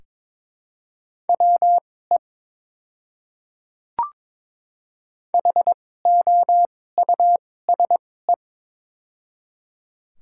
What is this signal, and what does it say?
Morse code, heard as a keyed tone, about 21 wpm, spaced as for 16 wpm, 700 Hz. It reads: WE HOUSE
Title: 2x Character Spacing - Top 300 Words - 22wpm